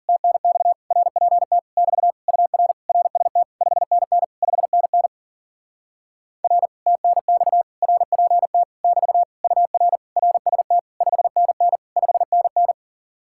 Morse code, heard as a keyed tone, about 53 wpm, 700 Hz. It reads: TNX RPT <BT> UR RST 5NN 5NN R TNX RPT <BT> UR RST 5NN 5NN